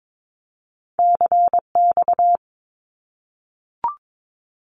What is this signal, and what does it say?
Morse code, heard as a keyed tone, about 22 wpm, 700 Hz. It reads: CX
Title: Abbreviations Repeated 3x - 22wpm